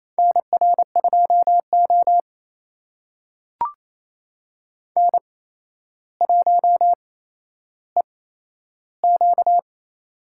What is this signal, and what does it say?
Morse code, heard as a keyed tone, about 28 wpm, spaced as for 12 wpm, 700 Hz. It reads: NR2O N1EQ
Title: USA Call Signs - 4 Characters - 8x Character Spacing - 28wpm